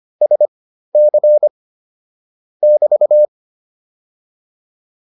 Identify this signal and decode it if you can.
Morse code, heard as a keyed tone, about 25 wpm, spaced as for 13 wpm, 600 Hz. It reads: SC <BT>